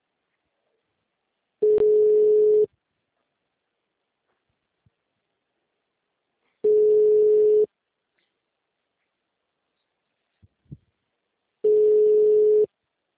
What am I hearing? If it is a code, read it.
Ringback tone (Europe, 425 Hz).